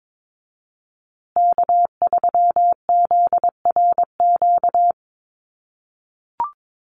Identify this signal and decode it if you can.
Morse code, heard as a keyed tone, about 22 wpm, 690 Hz. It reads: K3ZRQ
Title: USA Call Signs Repeated 3x 22wpm